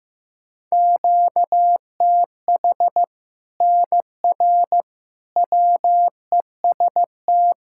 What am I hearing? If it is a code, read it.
Morse code, heard as a keyed tone, about 15 wpm, 700 Hz. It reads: QTH NR WEST
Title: Practice QSO - Exchange 1 - Beginning - 15wpm